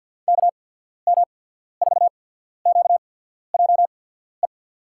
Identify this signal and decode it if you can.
Morse code, heard as a keyed tone, about 50 wpm, spaced as for 14 wpm, 700 Hz. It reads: KM4QJE
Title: USA Call Signs - 6 Characters - 8x Character Spacing - 50wpm